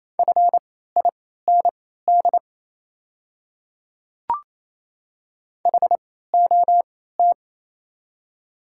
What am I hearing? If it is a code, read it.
Morse code, heard as a keyed tone, about 28 wpm, spaced as for 16 wpm, 700 Hz. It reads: FIND HOT
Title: 3x Character Spacing - Top 100 Words - 28wpm